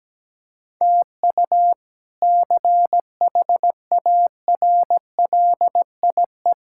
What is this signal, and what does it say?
Morse code, heard as a keyed tone, about 17 wpm, 700 Hz. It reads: TU CHARLIE
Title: K1USN SST QSO Practice - 17wpm